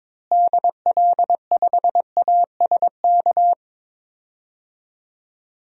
Morse code, heard as a keyed tone, about 22 wpm, 700 Hz. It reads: DL5ASK
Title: DXCC-100 Call Signs - 6 Characters - Repeated 3x - 22wpm